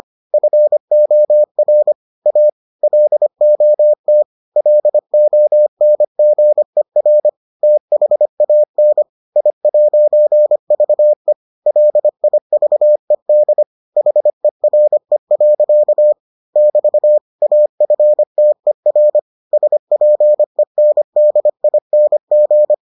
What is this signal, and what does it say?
Morse code, heard as a keyed tone, about 25 wpm, 600 Hz. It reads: FOR A LOT LONGER THAN I'VE LIVED HERE. <BT> AFTER SPENDING